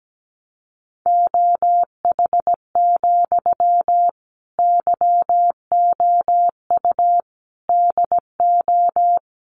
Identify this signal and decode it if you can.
Morse code, heard as a keyed tone, about 17 wpm, 700 Hz. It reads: OH, YOU DO
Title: Sentences from Top 600 Words 17wpm